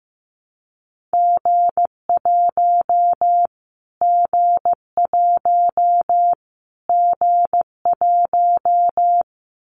Morse code, heard as a keyed tone, about 15 wpm, 700 Hz. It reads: G1 G1 G1
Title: Callsign Prefixes for Top 25 DXCC - Repeated 3x - 15wpm